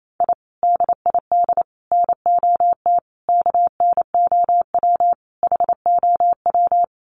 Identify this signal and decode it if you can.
Morse code, heard as a keyed tone, about 28 wpm, 700 Hz. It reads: I DID NOT KNOW HOW
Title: Sets of 5 Words Encore 28wpm